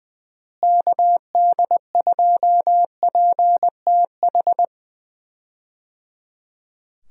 Morse code, heard as a keyed tone, about 20 wpm, 700 Hz. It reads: KD2PTH
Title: USA Call Signs - 6 Characters - Repeated 3x - 20wpm